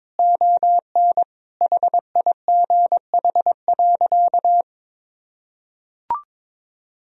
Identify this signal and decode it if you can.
Morse code, heard as a keyed tone, about 21 wpm, 700 Hz. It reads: ON HIGH.
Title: Sentences from Top 1000 Words - Review Part 1 - 22wpm